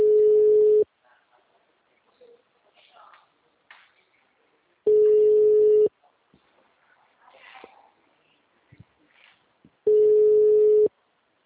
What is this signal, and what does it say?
Ringback tone (Europe, 425 Hz), with speech.